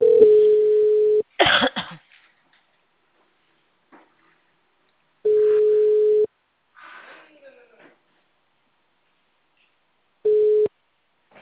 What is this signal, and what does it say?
Ringback tone (Europe, 425 Hz).
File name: out- -2006- - - .21980